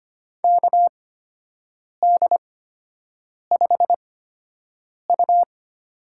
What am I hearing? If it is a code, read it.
Morse code, heard as a keyed tone, about 25 wpm, spaced as for 7 wpm, 700 Hz. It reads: KD5U
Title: USA Call Signs - 6 Characters - 8x Character Spacing - 25wpm